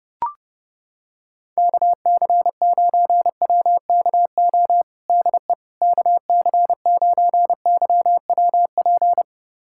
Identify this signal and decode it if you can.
Morse code, heard as a keyed tone, about 30 wpm, 700 Hz. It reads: KC9WKO DE KC9YWP
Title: USA Callsign DE Callsign 30wpm